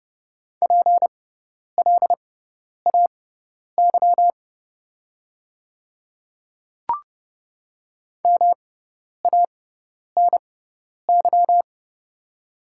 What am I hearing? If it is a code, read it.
Morse code, heard as a keyed tone, about 29 wpm, spaced as for 10 wpm, 700 Hz. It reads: PLAY MANY
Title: 6x Character Spacing - Top 500 Words - 30wpm